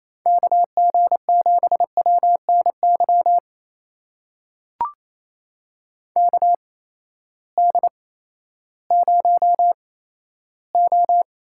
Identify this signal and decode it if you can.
Morse code, heard as a keyed tone, about 28 wpm, spaced as for 14 wpm, 700 Hz. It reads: KG7WNY KD0O
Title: USA Call Signs - 6 Characters - 8x Character Spacing - 28wpm